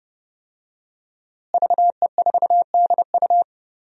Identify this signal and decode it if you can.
Morse code, heard as a keyed tone, about 30 wpm, 700 Hz. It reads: VE4DU
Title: DXCC-100 Call Signs - 5 Characters - Repeated 2x - 30wpm